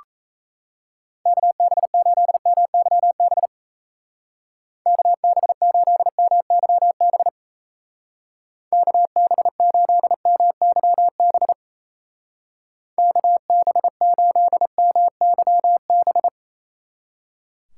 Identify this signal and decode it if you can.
Morse code, heard as a keyed tone, about 34 wpm, 700 Hz. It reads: KB8MYB KB8MYB KB8MYB KB8MYB
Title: Speed Racing - USA Call Signs 28wpm